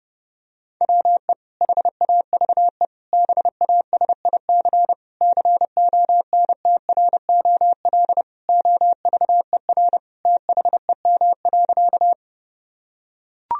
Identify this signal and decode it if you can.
Morse code, heard as a keyed tone, about 31 wpm, 700 Hz. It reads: WE HAVE BASIC CONTROL OVER THEM.
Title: Sentences from Top 1000 Words - Review Part 3 - 30wpm